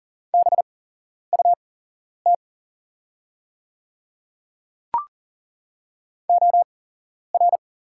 Morse code, heard as a keyed tone, about 40 wpm, spaced as for 11 wpm, 700 Hz. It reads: BUT OR
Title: 8x Character Spacing - Top 100 Words - 40wpm